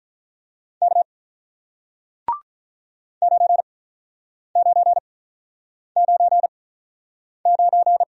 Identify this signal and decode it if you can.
Morse code, heard as a keyed tone, about 47 wpm, spaced as for 9 wpm, 700 Hz. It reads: K 9999